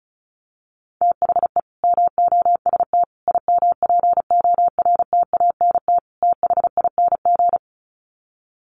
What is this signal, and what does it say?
Morse code, heard as a keyed tone, about 34 wpm, 700 Hz. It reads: THE MOST IMPORTANT THING
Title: Sets of 4 Words 35wpm